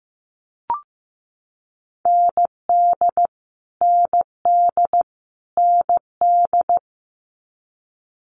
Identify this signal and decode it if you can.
Morse code, heard as a keyed tone, about 15 wpm, 700 Hz. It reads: ND ND ND
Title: Multi-letter Phonemes Repeated 3x 15wpm